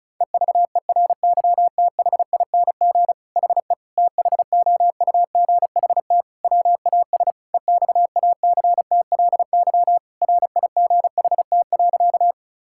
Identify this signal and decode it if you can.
Morse code, heard as a keyed tone, about 34 wpm, 700 Hz. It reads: EVERYTHING HE THOUGHT WAS EXACTLY RIGHT.
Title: Sentences from Top 1000 Words - Review Part 6 - 35wpm